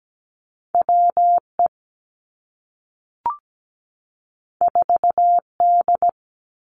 Morse code, heard as a keyed tone, about 17 wpm, 700 Hz. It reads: W E 4 D
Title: Two Challenging Letter-Number Combinations 17wpm